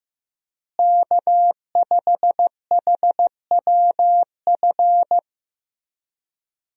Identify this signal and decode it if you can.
Morse code, heard as a keyed tone, about 15 wpm, 700 Hz. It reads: K5HWF
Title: USA Call Signs - 5 Characters - Repeated 3x - 15wpm